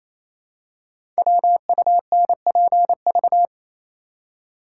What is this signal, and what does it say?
Morse code, heard as a keyed tone, about 27 wpm, 700 Hz. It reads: WUNPV
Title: Code Groups - 5 Letters - 28wpm